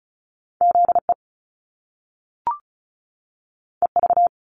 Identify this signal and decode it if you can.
Morse code, heard as a keyed tone, about 36 wpm, 700 Hz. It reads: ZE EV
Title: Top Two Letter Combinations in Words 35wpm